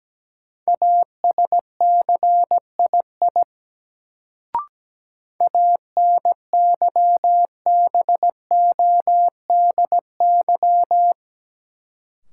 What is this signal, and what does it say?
Morse code, heard as a keyed tone, about 17 wpm, 700 Hz. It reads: ASCII ANYBODY
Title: Sets of 5 Words - New Word Drill - 17wpm